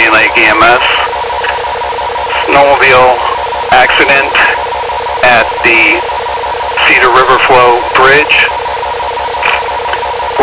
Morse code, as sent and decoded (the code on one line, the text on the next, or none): none